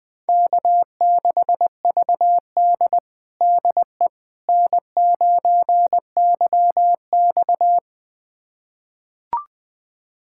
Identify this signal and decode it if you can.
Morse code, heard as a keyed tone, about 20 wpm, 700 Hz. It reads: K6VD DE N9YX
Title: USA Callsign DE Callsign - 4 Characters - 20wpm